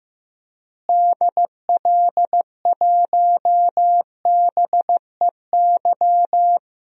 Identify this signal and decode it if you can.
Morse code, heard as a keyed tone, about 15 wpm, 700 Hz. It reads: DL1BEY
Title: DXCC-100 Call Signs - 6 Characters - Repeated 2x - 15wpm